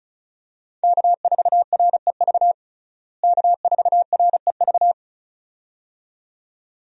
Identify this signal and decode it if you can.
Morse code, heard as a keyed tone, about 36 wpm, 700 Hz. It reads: K4REV K4REV
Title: USA Call Signs - 5 Characters - Repeated 2x - 35wpm